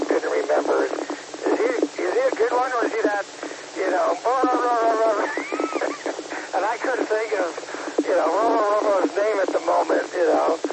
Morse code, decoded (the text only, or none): none